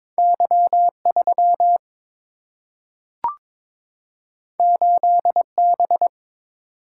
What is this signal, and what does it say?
Morse code, heard as a keyed tone, about 22 wpm, 700 Hz. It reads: Y3 8B
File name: practice / 22wpm / Two Letter-Number Combinations - 22wpm